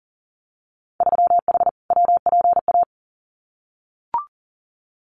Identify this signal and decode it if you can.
Morse code, heard as a keyed tone, about 38 wpm, 700 Hz. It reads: 3H WPA
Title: Winter Field Day Exchanges 40wpm